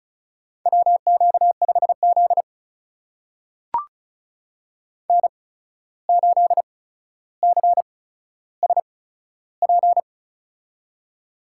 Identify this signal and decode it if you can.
Morse code, heard as a keyed tone, about 36 wpm, spaced as for 14 wpm, 700 Hz. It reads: WQ5Z N8CSP
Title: USA Call Signs - 8x Character Spacing - 35wpm